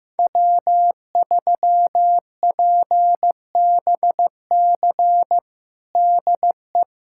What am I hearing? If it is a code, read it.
Morse code, heard as a keyed tone, about 15 wpm, 700 Hz. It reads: W3PBC DE
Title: USA Callsign DE Callsign 15wpm